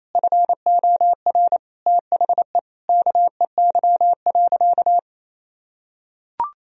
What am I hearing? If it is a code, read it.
Morse code, heard as a keyed tone, about 28 wpm, 700 Hz. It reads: FOR THE KEY.
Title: Sentences from Top 1000 Words - Review Part 4 - 28wpm